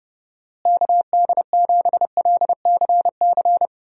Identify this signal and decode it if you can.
Morse code, heard as a keyed tone, about 31 wpm, 700 Hz. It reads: KD7LCC
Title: USA Call Signs - 6 Characters - Repeated 2x - 30wpm